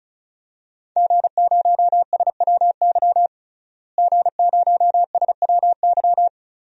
Morse code, heard as a keyed tone, about 36 wpm, 700 Hz. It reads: G0SWY G0SWY
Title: DXCC-100 Call Signs - 5 Characters - Repeated 2x - 35wpm